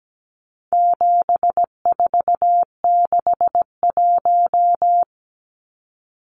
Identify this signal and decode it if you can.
Morse code, heard as a keyed tone, about 17 wpm, 700 Hz. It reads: 7461
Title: Numbers - 4-Digit 17wpm